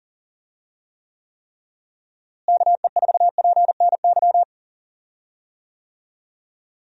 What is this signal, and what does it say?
Morse code, heard as a keyed tone, about 40 wpm, 700 Hz. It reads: KE4PNY
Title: USA Call Signs - 6 Characters - Repeated 3x - 40wpm